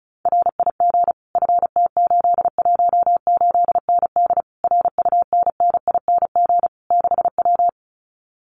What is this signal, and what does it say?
Morse code, heard as a keyed tone, about 34 wpm, 700 Hz. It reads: RIG FT818ND RUNNING 6W
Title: Common QSO Phrases Encore 35wpm